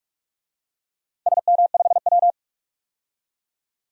Morse code, heard as a keyed tone, about 44 wpm, 700 Hz. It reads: SM5W
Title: DXCC-100 Call Signs - 4 Characters - 45wpm